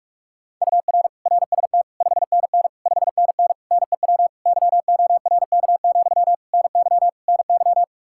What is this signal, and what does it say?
Morse code, heard as a keyed tone, about 46 wpm, 700 Hz. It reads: UR RST 5NN 5NN NEW YORK, NY NY
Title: Practice QSO - Exchange 1 - Beginning - 45wpm